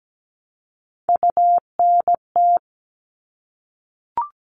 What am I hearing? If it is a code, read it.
Morse code, heard as a keyed tone, about 17 wpm, 700 Hz. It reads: UNT
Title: All Cut Numbers - 3-Digit 17wpm